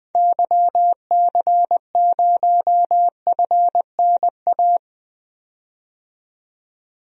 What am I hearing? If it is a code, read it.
Morse code, heard as a keyed tone, about 20 wpm, 700 Hz. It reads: YC0FNA
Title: DXCC-100 Call Signs - 6 Characters - 20wpm